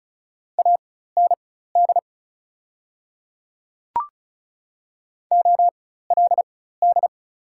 Morse code, heard as a keyed tone, about 36 wpm, spaced as for 16 wpm, 690 Hz. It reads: AND OLD
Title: 4x Character Spacing - Top 200 Words - 35wpm